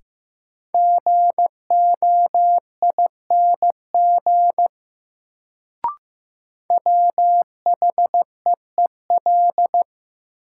Morse code, heard as a keyed tone, about 15 wpm, 700 Hz. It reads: GOING WHEEL
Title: Sentences from Top 500 Words - New Word Drill - 15wpm